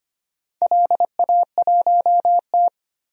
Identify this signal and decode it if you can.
Morse code, heard as a keyed tone, about 25 wpm, 700 Hz. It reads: LA1T
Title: DXCC-100 Call Signs - 4 Characters - 25wpm